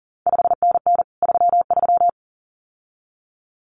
Morse code, heard as a keyed tone, about 39 wpm, 700 Hz. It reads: 5NN 33
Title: CQ WW DX Contest Exchange Encore - 40wpm